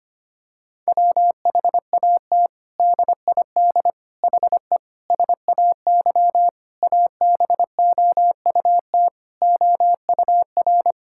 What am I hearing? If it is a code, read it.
Morse code, heard as a keyed tone, about 25 wpm, 700 Hz. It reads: WHAT DID HE SAY ABOUT OUR
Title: Sentences from Top 1000 Words - Review Part 6 - 25wpm